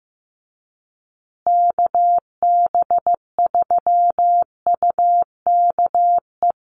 Morse code, heard as a keyed tone, about 15 wpm, 700 Hz. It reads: KB3UKE
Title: USA Call Signs Repeated 2x 15wpm